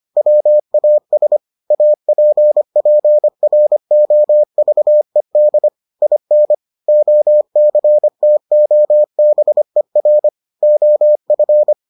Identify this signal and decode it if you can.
Morse code, heard as a keyed tone, about 25 wpm, 600 Hz. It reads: WAS APPROVED IN OCTOBER OF